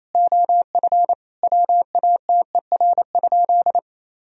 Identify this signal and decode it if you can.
Morse code, heard as a keyed tone, about 28 wpm, 700 Hz. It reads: OF WATER?
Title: Sentences from Top 1000 Words - Review Part 2 - 28wpm